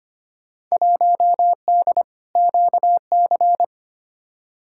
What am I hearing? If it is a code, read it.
Morse code, heard as a keyed tone, about 25 wpm, 700 Hz. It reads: 1D QC
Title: Field Day Exchanges - 25wpm